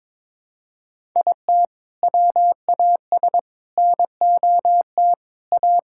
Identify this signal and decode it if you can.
Morse code, heard as a keyed tone, about 22 wpm, 700 Hz. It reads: IT WAS NOT A
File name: practice / 22wpm / Sentences with 4-Word Phrases - 22wpm